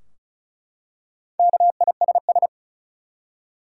Morse code, heard as a keyed tone, about 36 wpm, 700 Hz. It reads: KISS
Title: Four Letter Words - 35wpm